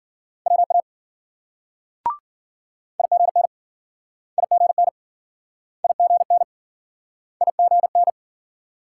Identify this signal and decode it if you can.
Morse code, heard as a keyed tone, about 53 wpm, 700 Hz. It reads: WA IGN IGN IGN IGN